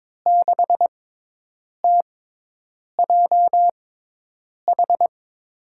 Morse code, heard as a keyed tone, about 21 wpm, spaced as for 8 wpm, 700 Hz. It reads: 6TJH
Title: USA Call Signs - 5 Characters - 6x Character Spacing - 22wpm